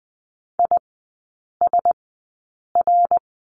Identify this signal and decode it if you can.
Morse code, heard as a keyed tone, about 20 wpm, spaced as for 8 wpm, 700 Hz. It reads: ISR
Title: 107 Lesson 18e - Letter m - Three Characters 20wpm